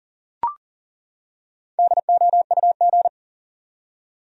Morse code, heard as a keyed tone, about 40 wpm, 700 Hz. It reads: DOUG
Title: USA Common Names - Max 4 Letters - 40wpm